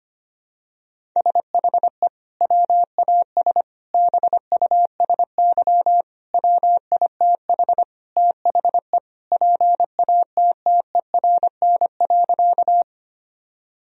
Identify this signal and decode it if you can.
Morse code, heard as a keyed tone, about 25 wpm, 700 Hz. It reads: SHE WAS BUSY WITH THE PATTERN.
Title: Sentences from Top 1000 Words - Review Part 2 - 25wpm